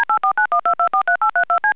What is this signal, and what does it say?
DTMF, heard as telephone touch tones, tones of about 100 ms. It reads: D84912243832C
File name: CallerID-DTMF